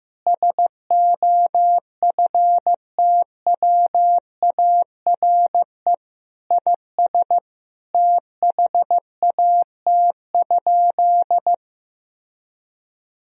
Morse code, heard as a keyed tone, about 15 wpm, 700 Hz. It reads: SOFTWARE IS THAT?
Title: Sentences from Top 1000 Words - Review Part 3 - 15wpm